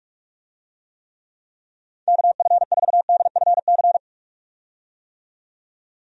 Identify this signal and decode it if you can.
Morse code, heard as a keyed tone, about 43 wpm, 700 Hz. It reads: KF4DFC
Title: USA Call Signs - 6 Characters - 45wpm